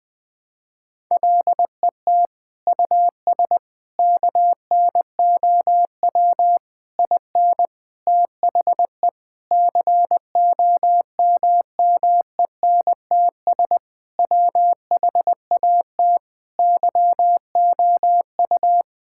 Morse code, heard as a keyed tone, about 20 wpm, 700 Hz. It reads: LET US KNOW IN THE COMMENTS WHAT YOU
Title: Sentences with 6-Word Phrases - 20wpm